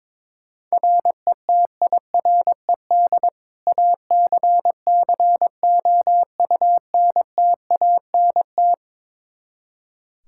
Morse code, heard as a keyed tone, about 22 wpm, 700 Hz. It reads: RETIRED ACCOUNTANT
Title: Practice QSO - Exchange 3 - More Info - 22wpm